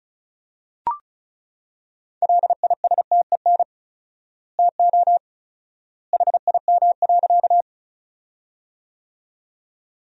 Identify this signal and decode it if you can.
Morse code, heard as a keyed tone, about 36 wpm, 700 Hz. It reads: LISTEN TO HIM.
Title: 4x Word-Spacing - Sentences from Top 400 Words - 35wpm